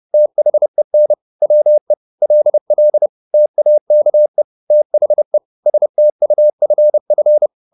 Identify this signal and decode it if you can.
Morse code, heard as a keyed tone, about 30 wpm, 600 Hz. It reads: THEN WE LL TAKE THE STUFF
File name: the old man and the sea / the old man and the sea 1 30wpm